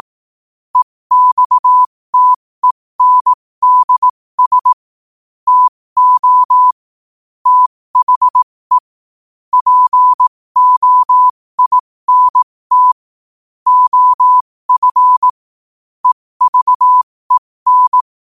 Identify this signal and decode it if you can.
Morse code, heard as a keyed tone, about 18 wpm, spaced as for 15 wpm, 1000 Hz. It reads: EXTENDS TO THE POINT OF EVEN
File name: qotdfiles / 15wpm